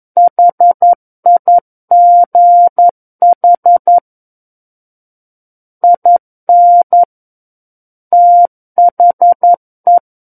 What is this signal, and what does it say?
Morse code, heard as a keyed tone, about 11 wpm, 700 Hz. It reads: HIGH IN THE
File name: 11 WPM